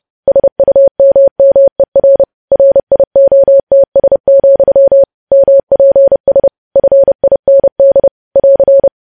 Morse code, heard as a keyed tone, about 30 wpm, 550 Hz. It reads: SUMMER RIOTS, MPS FIND <AR>
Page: news headlines in morse code at 30 wpm